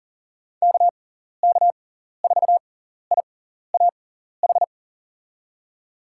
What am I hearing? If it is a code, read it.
Morse code, heard as a keyed tone, about 40 wpm, spaced as for 14 wpm, 700 Hz. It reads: KK4IAH
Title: USA Call Signs - 6 Characters - 6x Character Spacing - 40wpm